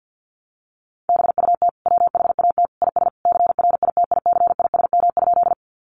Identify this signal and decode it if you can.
Morse code, heard as a keyed tone, about 48 wpm, 700 Hz. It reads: BUT WHAT IS CRITICISM?
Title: Sentences from Top 1300 Words 50wpm